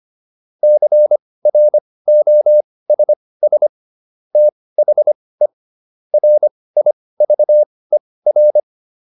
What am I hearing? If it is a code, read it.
Morse code, heard as a keyed tone, about 25 wpm, spaced as for 18 wpm, 600 Hz. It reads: CROSS THE RIVER